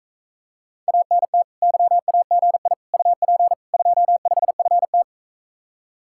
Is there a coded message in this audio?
Yes: Morse code from 0.9 to 5.0 s, heard as a keyed tone, about 45 wpm, 700 Hz. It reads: ANT YAGI UP 25FT